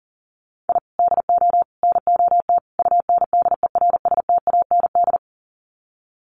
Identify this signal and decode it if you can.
Morse code, heard as a keyed tone, about 40 wpm, 700 Hz. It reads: I DO NOT UNDERSTAND
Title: Sets of 4 Words 40wpm